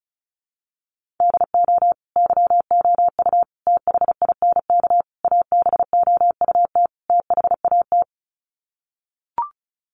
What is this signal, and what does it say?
Morse code, heard as a keyed tone, about 34 wpm, 700 Hz. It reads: DO YOU THINK ABOUT THAT
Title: Sets of 5 Words Encore2 35wpm